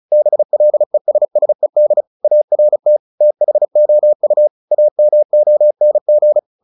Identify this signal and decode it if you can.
Morse code, heard as a keyed tone, about 34 wpm, 600 Hz. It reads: BLESSED ART THOU AMONG
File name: the old man and the sea / the old man and the sea 1 35wpm